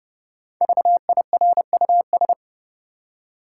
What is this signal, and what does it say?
Morse code, heard as a keyed tone, about 32 wpm, 700 Hz. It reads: VIRUS
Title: Five Letter Words - 30wpm